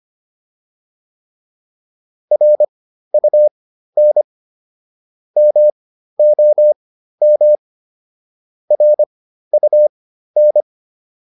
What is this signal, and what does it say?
Morse code, heard as a keyed tone, about 25 wpm, spaced as for 13 wpm, 600 Hz. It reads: RUN MOM RUN
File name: PR202 13